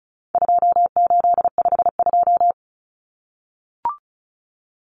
Morse code, heard as a keyed tone, about 34 wpm, 700 Hz. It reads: 2852
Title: Common Cut Numbers - 4-Digit 35wpm